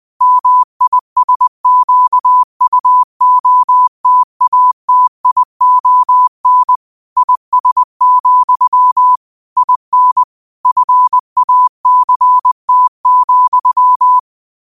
Morse code, heard as a keyed tone, about 20 wpm, 1000 Hz. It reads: MISQUOTATION IS, IN FACT,